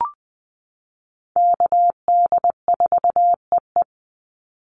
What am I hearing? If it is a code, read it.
Morse code, heard as a keyed tone, about 20 wpm, 700 Hz. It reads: KD4EE
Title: Call Signs - 20wpm